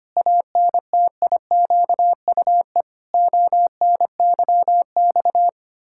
Morse code, heard as a keyed tone, about 25 wpm, 700 Hz. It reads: ANTIQUE ONYX
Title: Pangram Challenge v2 - 25wpm